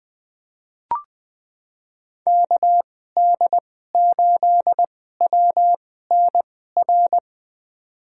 Morse code, heard as a keyed tone, about 20 wpm, spaced as for 14 wpm, 700 Hz. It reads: KD8WNR
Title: USA Call Signs - 6 Characters - 2x Character Spacing - 20wpm